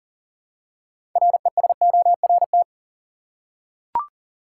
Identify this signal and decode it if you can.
Morse code, heard as a keyed tone, about 37 wpm, 700 Hz. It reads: RESORT
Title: Six Letter Words - 40wpm